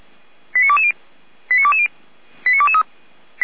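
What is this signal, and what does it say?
Five-tone selective call (ZVEI), heard as quick runs of five tones, tones of about 70 ms. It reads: 89220 89220 89223
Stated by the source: Received by SDR; ZVEI five-tone code 89187.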